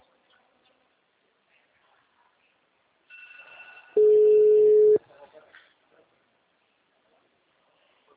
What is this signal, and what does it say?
Ringback tone (Europe, 425 Hz), with speech.
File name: out- -2009- - - .17